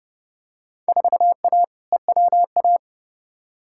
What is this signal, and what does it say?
Morse code, heard as a keyed tone, about 30 wpm, 700 Hz. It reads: 4A EWA
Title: Field Day Exchanges Encore - 30wpm